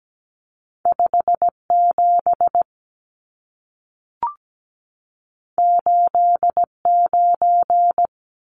Morse code, heard as a keyed tone, about 17 wpm, 700 Hz. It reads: 57 89
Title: Numbers - 2-Digit 17wpm